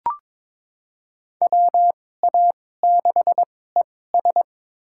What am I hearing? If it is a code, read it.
Morse code, heard as a keyed tone, about 22 wpm, spaced as for 16 wpm, 700 Hz. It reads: WA6ES
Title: USA Call Signs - 5 Characters - 2x Character Spacing - 22wpm